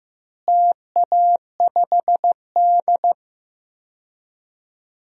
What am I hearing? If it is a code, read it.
Morse code, heard as a keyed tone, about 15 wpm, 700 Hz. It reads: TA5D
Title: DXCC-100 Call Signs - 4 Characters - Repeated 2x - 15wpm